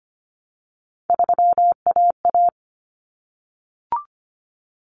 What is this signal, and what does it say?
Morse code, heard as a keyed tone, about 25 wpm, 700 Hz. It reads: 3AA
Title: Common Cut Numbers - 3-Digit 25wpm